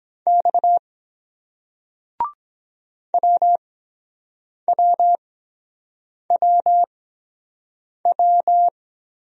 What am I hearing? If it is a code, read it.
Morse code, heard as a keyed tone, about 23 wpm, spaced as for 7 wpm, 700 Hz. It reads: X WWWW